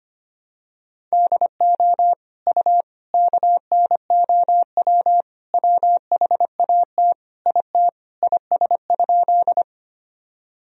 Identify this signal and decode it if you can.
Morse code, heard as a keyed tone, about 25 wpm, 700 Hz. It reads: DO U KNOW WHAT IT IS?
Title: Sentences from Top 100 Words and Abbreviations 25wpm